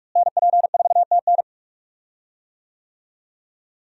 Morse code, heard as a keyed tone, about 43 wpm, 700 Hz. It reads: NP4TN
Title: USA Call Signs - 5 Characters - Repeated 2x - 45wpm